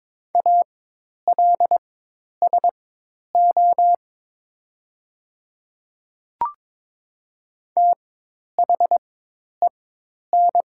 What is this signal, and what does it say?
Morse code, heard as a keyed tone, about 22 wpm, spaced as for 10 wpm, 700 Hz. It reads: ALSO THEN
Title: 4x Character Spacing - Top 200 Words - 22wpm